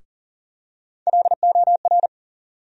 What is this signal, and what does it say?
Morse code, heard as a keyed tone, about 40 wpm, 700 Hz. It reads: LOR
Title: USA Call Sign Postfixes - 3 Letters 40wpm